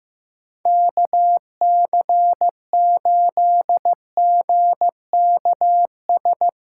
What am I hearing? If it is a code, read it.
Morse code, heard as a keyed tone, about 15 wpm, 700 Hz. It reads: KC8GKS
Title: USA Callsign DE Callsign 15wpm